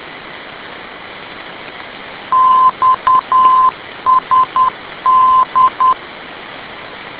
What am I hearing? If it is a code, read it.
Morse code, heard as a keyed tone, about 10 wpm, 1030 Hz. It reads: XSD
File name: info / xsd ndb